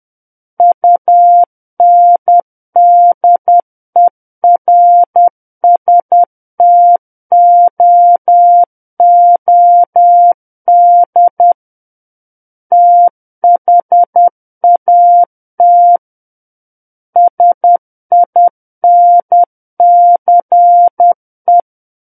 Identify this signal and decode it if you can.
Morse code, heard as a keyed tone, about 10 wpm, 700 Hz. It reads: UNDERSTOOD THAT SINCE
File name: Book One / 01-Chapter 1 - 10 WPM